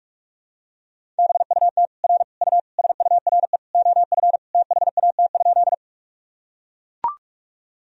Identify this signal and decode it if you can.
Morse code, heard as a keyed tone, about 41 wpm, 700 Hz. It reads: BUT R U SURE OF THAT?
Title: Sentences from Top 300 Words and Abbreviations 45wpm